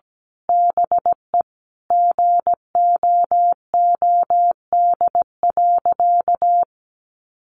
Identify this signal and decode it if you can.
Morse code, heard as a keyed tone, about 17 wpm, 700 Hz. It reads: BE GOOD.
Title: Sentences from Top 1500 Words 17wpm